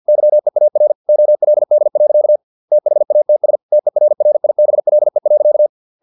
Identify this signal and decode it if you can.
Morse code, heard as a keyed tone, about 50 wpm, 600 Hz. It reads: YEAR OLD. THATS TERRIBLE.
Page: Never 49wpm